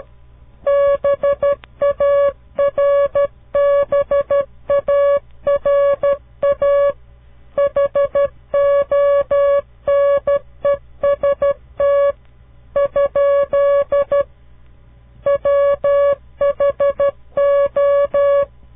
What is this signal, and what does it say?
Morse code, heard as a keyed tone, about 12 wpm, 560 Hz. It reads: BARBARA HONEST ? WHO
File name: CW Learning Programs / farnsworth / unit 2